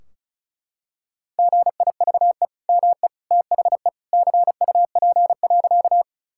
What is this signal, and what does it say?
Morse code, heard as a keyed tone, about 36 wpm, 700 Hz. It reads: GIVE ME THE CUP.
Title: Sentences from Top 1600 Words 35wpm